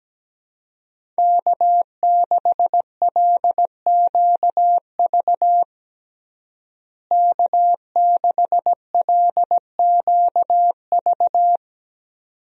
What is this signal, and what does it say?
Morse code, heard as a keyed tone, about 17 wpm, 700 Hz. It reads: K6LQV K6LQV
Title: USA Call Signs - 5 Characters - Repeated 2x - 17wpm